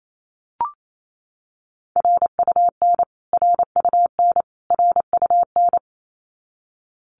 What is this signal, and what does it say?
Morse code, heard as a keyed tone, about 28 wpm, 700 Hz. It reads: RUN RUN RUN
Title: Top 500 Words Repeated 28wpm